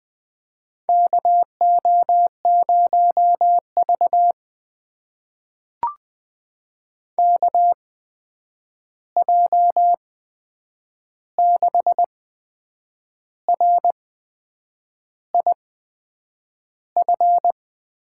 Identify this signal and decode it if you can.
Morse code, heard as a keyed tone, about 20 wpm, spaced as for 8 wpm, 700 Hz. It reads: KO0V KJ6RIF
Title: USA Call Signs - 8x Character Spacing - 20wpm